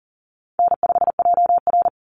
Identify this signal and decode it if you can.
Morse code, heard as a keyed tone, about 40 wpm, 700 Hz. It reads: N5JR
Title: Call Signs Repeated 3x - 40wpm